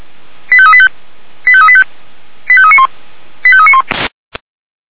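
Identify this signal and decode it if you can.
Five-tone selective call (ZVEI), heard as quick runs of five tones, tones of about 70 ms. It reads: 85385 85385 85391 85391
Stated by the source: ZVEI five-tone call received by SDR, 464.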